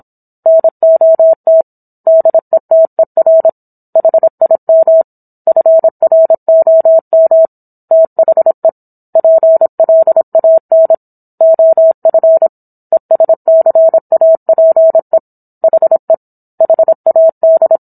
Morse code, heard as a keyed tone, about 27 wpm, 650 Hz. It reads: NOT DETER HIM FROM THE PLAN OF ESCAPE HE HAD